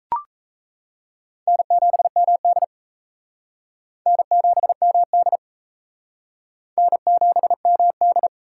Morse code, heard as a keyed tone, about 40 wpm, 700 Hz. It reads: N7MD N7MD N7MD